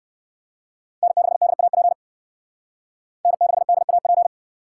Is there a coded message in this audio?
Yes: Morse code from 1.0 to 4.3 s, heard as a keyed tone, about 67 wpm, 700 Hz. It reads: N6DAP N6DAP